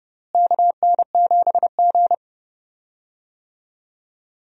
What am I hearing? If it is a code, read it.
Morse code, heard as a keyed tone, about 32 wpm, 700 Hz. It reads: KN7G
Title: USA Call Signs - 4 Characters - Repeated 2x - 30wpm